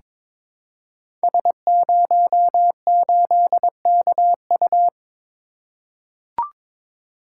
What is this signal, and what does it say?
Morse code, heard as a keyed tone, about 22 wpm, 700 Hz. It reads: S08KU
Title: Code Groups - 5 Letter:Numbers - 22wpm